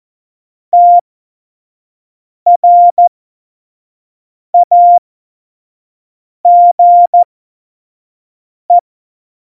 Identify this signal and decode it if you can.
Morse code, heard as a keyed tone, about 14 wpm, spaced as for 5 wpm, 700 Hz. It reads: TRAGE
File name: Morsekurs / Witze / Witz 20. August 2019, 25 F70